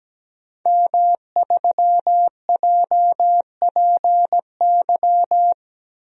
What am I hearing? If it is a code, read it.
Morse code, heard as a keyed tone, about 17 wpm, 700 Hz. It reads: M3JPY
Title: DXCC-100 Call Signs - 5 Characters - Repeated 3x - 17wpm